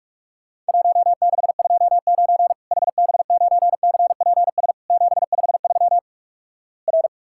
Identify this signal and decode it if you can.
Morse code, heard as a keyed tone, about 44 wpm, 700 Hz. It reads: 1629 HB9CPS 753 R
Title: Worked All Europe DX Contest - QTC Exchange - 45wpm